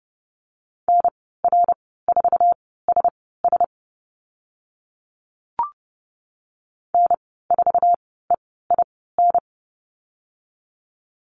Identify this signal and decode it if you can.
Morse code, heard as a keyed tone, about 30 wpm, spaced as for 17 wpm, 700 Hz. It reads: NR4SS N4EIN
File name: Lesson 14g - Number 2 - Call Signs 30wpm